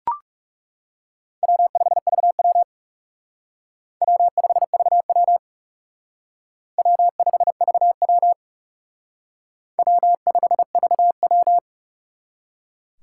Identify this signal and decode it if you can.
Morse code, heard as a keyed tone, about 36 wpm, 700 Hz. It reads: W5VW W5VW W5VW W5VW
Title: Speed Racing - USA 1x2 Call Signs 30wpm